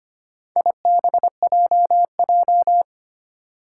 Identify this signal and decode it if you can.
Morse code, heard as a keyed tone, about 25 wpm, 700 Hz. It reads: IBJJ
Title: Four Challenging Letter-Number Combinations 25wpm